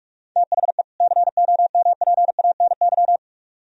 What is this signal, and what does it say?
Morse code, heard as a keyed tone, about 44 wpm, 700 Hz. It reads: THE COMPANY